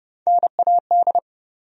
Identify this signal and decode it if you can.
Morse code, heard as a keyed tone, about 30 wpm, 700 Hz. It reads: NAD
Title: USA Call Sign Postfixes - 3 Letters 30wpm